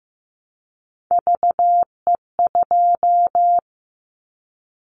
Morse code, heard as a keyed tone, about 14 wpm, 700 Hz. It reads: VE2
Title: Callsign Prefixes for Top 50 DXCC - 15wpm